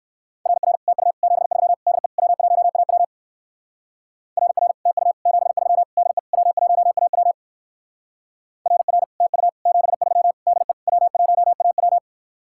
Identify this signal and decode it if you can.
Morse code, heard as a keyed tone, about 59 wpm, 700 Hz. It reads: RR TU 73 DE W1AW RR TU 73 DE W1AW RR TU 73 DE W1AW